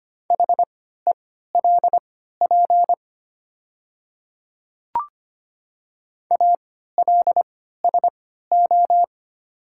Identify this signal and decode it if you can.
Morse code, heard as a keyed tone, about 25 wpm, spaced as for 14 wpm, 700 Hz. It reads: HELP ALSO
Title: 3x Character Spacing - Top 200 Words - 25wpm